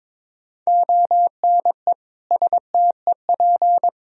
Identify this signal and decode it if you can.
Morse code, heard as a keyed tone, about 22 wpm, 700 Hz. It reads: ONE STEP
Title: Sentences from Top 1000 Words - Review Part 3 - 22wpm